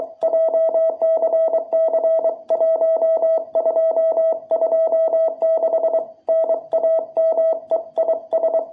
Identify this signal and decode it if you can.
Morse code, heard as a keyed tone, about 23 wpm, 660 Hz. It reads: JCC1226 NAME IS
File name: jh8cbh / 2021cw